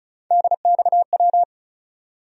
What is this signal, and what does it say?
Morse code, heard as a keyed tone, about 36 wpm, 700 Hz. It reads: DXW
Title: USA Call Sign Postfixes - 3 Letters 35wpm